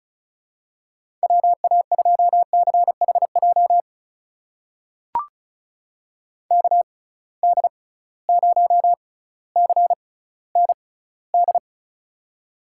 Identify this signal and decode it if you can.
Morse code, heard as a keyed tone, about 36 wpm, spaced as for 18 wpm, 700 Hz. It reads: WA2CHJ KD0CND
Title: USA Call Signs - 6 Characters - 6x Character Spacing - 35wpm